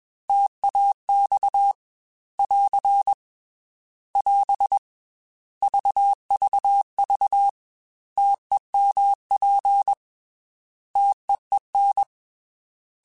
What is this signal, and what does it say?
Morse code, heard as a keyed tone, about 21 wpm, 800 Hz. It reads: TAX <AR> <AS> VVV TEMP TEEN
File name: T Prefixes / T mp3s